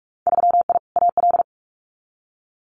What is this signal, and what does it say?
Morse code, heard as a keyed tone, about 46 wpm, 700 Hz. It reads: 3I AL
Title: Winter Field Day Exchanges Encore 45wpm